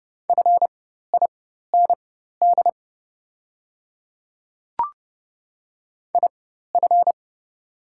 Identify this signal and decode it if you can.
Morse code, heard as a keyed tone, about 30 wpm, spaced as for 14 wpm, 700 Hz. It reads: FIND IF
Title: 4x Character Spacing - Top 100 Words - 30wpm